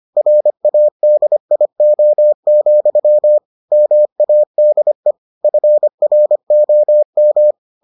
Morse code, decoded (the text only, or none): RADIO, MADE FROM